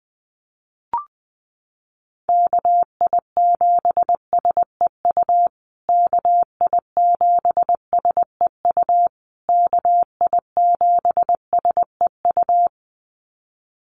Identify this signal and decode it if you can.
Morse code, heard as a keyed tone, about 20 wpm, 700 Hz. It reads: KI7SEU KI7SEU KI7SEU
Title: USA Call Signs Repeated 3x 20wpm